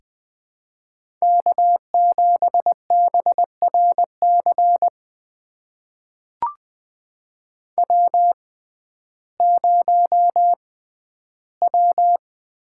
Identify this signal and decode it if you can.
Morse code, heard as a keyed tone, about 20 wpm, spaced as for 12 wpm, 700 Hz. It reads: K7BRC W0W
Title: USA Call Signs - 5 Characters - 6x Character Spacing - 20wpm